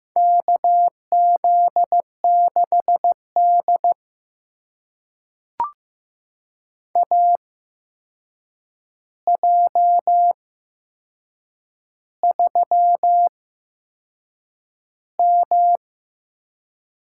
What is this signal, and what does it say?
Morse code, heard as a keyed tone, about 15 wpm, spaced as for 6 wpm, 700 Hz. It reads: KZ6D AJ3M
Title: USA Call Signs - 4 Characters - 8x Character Spacing - 15wpm